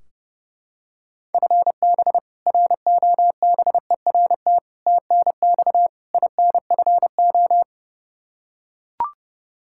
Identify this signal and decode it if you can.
Morse code, heard as a keyed tone, about 32 wpm, 700 Hz. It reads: FB ROBERT TNX INFO
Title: K1USN SST QSO Practice 30wpm